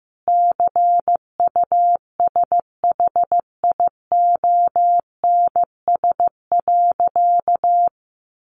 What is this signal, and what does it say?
Morse code, heard as a keyed tone, about 15 wpm, 700 Hz. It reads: CUSHIONS.
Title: A Sound Of Thunder 15wpm